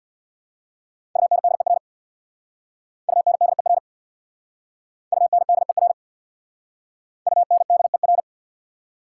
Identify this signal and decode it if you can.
Morse code, heard as a keyed tone, about 63 wpm, 690 Hz. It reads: UNDER UNDER UNDER UNDER